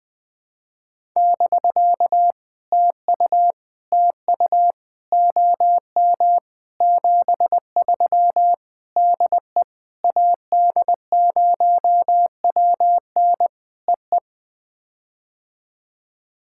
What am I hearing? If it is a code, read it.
Morse code, heard as a keyed tone, about 20 wpm, 700 Hz. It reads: <BK> TU TU OM 73 DE AD0WN EE
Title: POTA QSO Practice - 20wpm